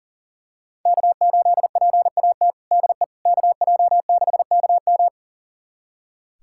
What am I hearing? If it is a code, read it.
Morse code, heard as a keyed tone, about 40 wpm, 700 Hz. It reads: K8PAT DE KJ6KM
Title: USA Callsign DE Callsign - 5 Characters - 40wpm